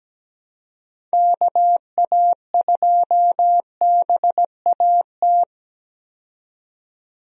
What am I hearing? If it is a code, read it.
Morse code, heard as a keyed tone, about 17 wpm, 700 Hz. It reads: KA2BAT
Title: USA Call Signs - 6 Characters - 17wpm